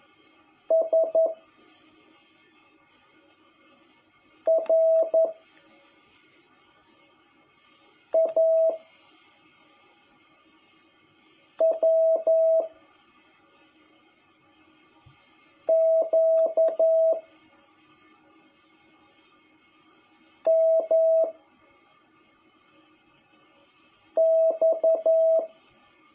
Morse code, heard as a keyed tone, about 11 wpm, 660 Hz. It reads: S R A W Q M X
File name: jh8cbh / 2021cw